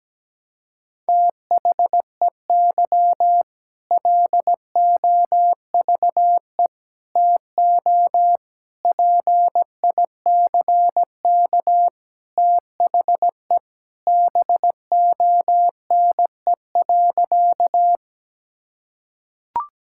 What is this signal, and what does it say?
Morse code, heard as a keyed tone, about 17 wpm, 700 Hz. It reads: THEY LOVE TO PICK THE BONE.
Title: Sentences from Top 1600 Words 17wpm